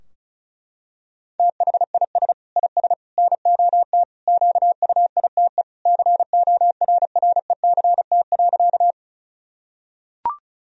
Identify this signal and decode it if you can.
Morse code, heard as a keyed tone, about 34 wpm, 700 Hz. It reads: THIS IS NOT QUITE CORRECT.
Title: Sentences from Top 1000 Words - Review Part 2 - 35wpm